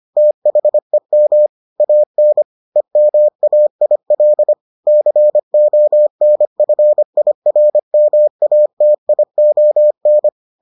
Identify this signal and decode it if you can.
Morse code, heard as a keyed tone, about 25 wpm, 600 Hz. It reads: THEM AN EMAIL CONFIRMATION